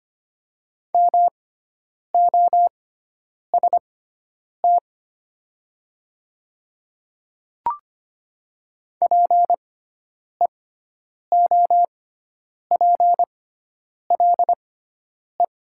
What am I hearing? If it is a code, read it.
Morse code, heard as a keyed tone, about 25 wpm, spaced as for 9 wpm, 700 Hz. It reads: MOST PEOPLE
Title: 6x Character Spacing - Top 100 Words - 25wpm